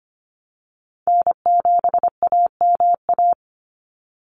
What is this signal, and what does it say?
Morse code, heard as a keyed tone, about 25 wpm, 700 Hz. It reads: N7AMA
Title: NA SOTA Chaser Callsigns 25wpm